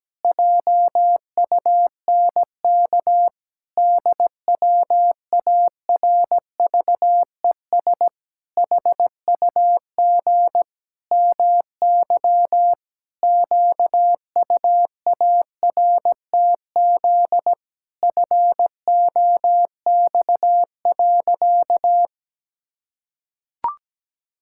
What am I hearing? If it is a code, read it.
Morse code, heard as a keyed tone, about 17 wpm, 700 Hz. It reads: JUNK DWARVES HUG MY QUARTZ FOX.
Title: Pangram Challenge v2 - 17wpm